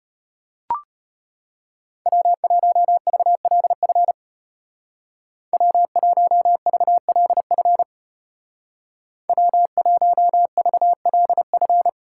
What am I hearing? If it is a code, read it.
Morse code, heard as a keyed tone, about 34 wpm, 700 Hz. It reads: W1VLF W1VLF W1VLF